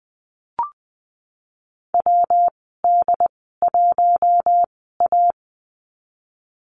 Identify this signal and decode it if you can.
Morse code, heard as a keyed tone, about 20 wpm, spaced as for 15 wpm, 700 Hz. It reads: WD1A
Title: 161 Lesson 26g - Letter g - Call Signs 20wpm